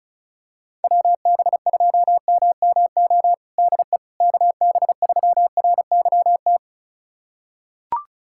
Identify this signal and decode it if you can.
Morse code, heard as a keyed tone, about 36 wpm, 690 Hz. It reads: WB2MMO DE KB3RYT
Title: USA Callsign DE Callsign - 6 Characters - 35wpm